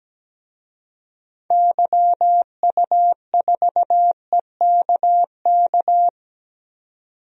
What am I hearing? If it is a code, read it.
Morse code, heard as a keyed tone, about 17 wpm, 700 Hz. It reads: YU4EKK
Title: DXCC-100 Call Signs - 6 Characters - 17wpm